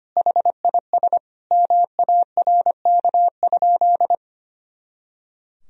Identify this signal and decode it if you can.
Morse code, heard as a keyed tone, about 25 wpm, 700 Hz. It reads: HIS MARK?
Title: Sentences from Top 300 Words and Abbreviations 25wpm